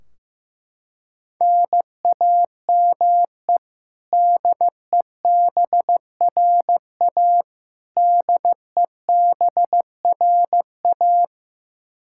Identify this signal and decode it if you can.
Morse code, heard as a keyed tone, about 15 wpm, 700 Hz. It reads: NAME DEBRA DEBRA
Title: Practice QSO - Exchange 1 - Beginning - 15wpm